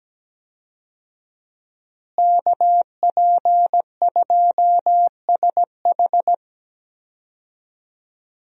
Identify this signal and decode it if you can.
Morse code, heard as a keyed tone, about 17 wpm, 700 Hz. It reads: KP2SH
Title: DXCC-100 Call Signs - 5 Characters - Repeated 2x - 17wpm